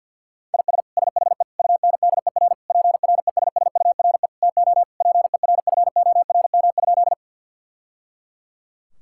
Morse code, heard as a keyed tone, about 50 wpm, 700 Hz. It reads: IS SHE UNDER PRESSURE TO PERFORM?
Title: Sentences from Top 1000 Words - Review Part 6 - 50wpm